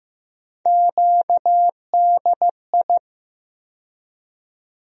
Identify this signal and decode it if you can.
Morse code, heard as a keyed tone, about 15 wpm, 700 Hz. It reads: QDI
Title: USA Call Sign Postfixes - 3 Letters 15wpm